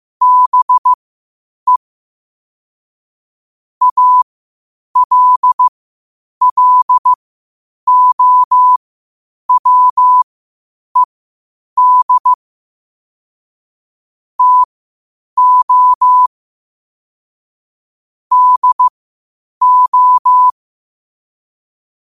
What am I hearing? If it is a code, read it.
Morse code, heard as a keyed tone, about 15 wpm, spaced as for 8 wpm, 1000 Hz. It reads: BE ALLOWED TO DO
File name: qotdfiles / 8wpm